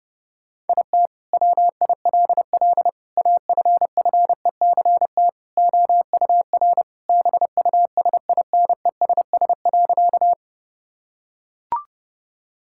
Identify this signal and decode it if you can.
Morse code, heard as a keyed tone, about 32 wpm, 700 Hz. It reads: IT WILL AFFECT OUR BUSINESS.
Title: Sentences from Top 1000 Words - Review Part 6 - 30wpm